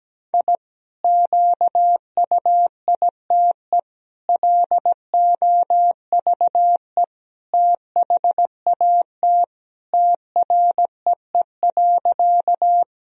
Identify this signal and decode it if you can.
Morse code, heard as a keyed tone, about 17 wpm, 700 Hz. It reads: I QUITE LOVE THAT TREE.
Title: Sentences from Top 1000 Words - Review Part 1 - 17wpm